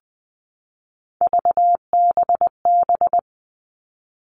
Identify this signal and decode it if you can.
Morse code, heard as a keyed tone, about 20 wpm, 700 Hz. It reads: VBB
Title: Three Challenging Letter-Number Combinations 20wpm